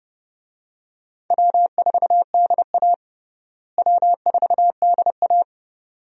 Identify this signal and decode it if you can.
Morse code, heard as a keyed tone, about 29 wpm, 700 Hz. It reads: W4DA W4DA
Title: USA Call Signs - 4 Characters - Repeated 2x - 30wpm